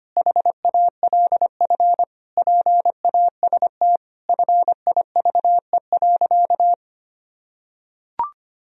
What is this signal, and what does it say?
Morse code, heard as a keyed tone, about 25 wpm, 700 Hz. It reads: HALF PAST FIVE.
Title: Sentences from Top 1000 Words - Review Part 3 - 25wpm